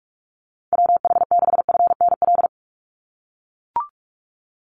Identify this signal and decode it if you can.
Morse code, heard as a keyed tone, about 44 wpm, 700 Hz. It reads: WH6FNL
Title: USA Call Signs 45wpm